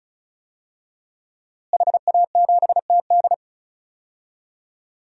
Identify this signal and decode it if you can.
Morse code, heard as a keyed tone, about 34 wpm, 700 Hz. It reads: HA7TD
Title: DXCC-100 Call Signs - 5 Characters - Repeated 3x - 35wpm